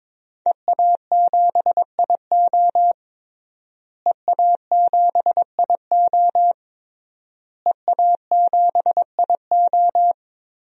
Morse code, heard as a keyed tone, about 22 wpm, 700 Hz. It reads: EA7IO EA7IO EA7IO
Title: DXCC-100 Call Signs - 5 Characters - Repeated 3x - 22wpm